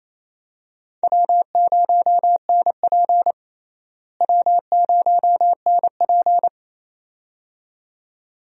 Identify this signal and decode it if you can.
Morse code, heard as a keyed tone, about 28 wpm, 700 Hz. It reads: W0NP W0NP
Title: USA Call Signs - 4 Characters - Repeated 2x - 28wpm